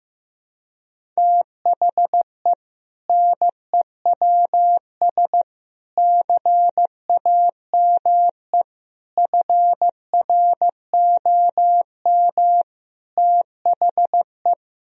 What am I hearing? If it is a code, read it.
Morse code, heard as a keyed tone, about 16 wpm, 700 Hz. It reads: THE NEWS CAME FROM THE
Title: Sentences from Top 1000 Words - Review Part 1 - 15wpm